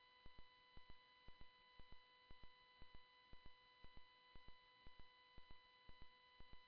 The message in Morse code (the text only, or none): none